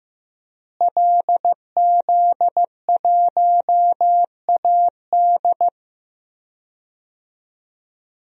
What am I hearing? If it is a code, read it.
Morse code, heard as a keyed tone, about 15 wpm, 700 Hz. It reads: LZ1AD
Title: DXCC-100 Call Signs - 5 Characters - Repeated 2x - 15wpm